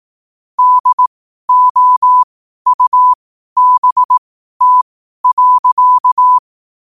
Morse code, heard as a keyed tone, about 18 wpm, spaced as for 12 wpm, 1000 Hz. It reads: DOUBT.